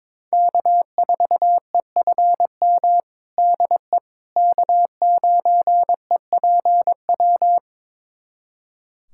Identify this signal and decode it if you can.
Morse code, heard as a keyed tone, about 22 wpm, 700 Hz. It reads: K4EFM DE K9EPW
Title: USA Callsign DE Callsign - 5 Characters - 22wpm